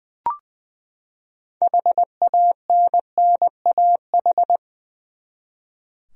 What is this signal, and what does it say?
Morse code, heard as a keyed tone, about 21 wpm, 700 Hz. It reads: HANNAH
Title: USA Common Names - 20wpm